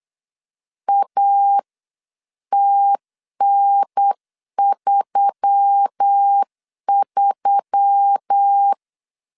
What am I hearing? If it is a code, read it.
Morse code, heard as a keyed tone, about 8 wpm, 800 Hz. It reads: ATN33
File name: CW / DET / 1 6 1500C S C 8wpm 5219